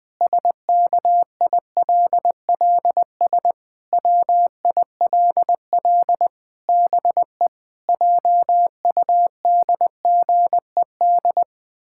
Morse code, heard as a keyed tone, about 20 wpm, 700 Hz. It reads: SKILLS WILL BE JUDGED
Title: Pangram Challenge v2 - 20wpm